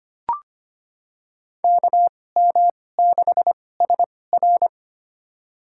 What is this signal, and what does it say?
Morse code, heard as a keyed tone, about 25 wpm, spaced as for 18 wpm, 700 Hz. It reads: KM6SR
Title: USA Call Signs - 5 Characters - 2x Character Spacing - 25wpm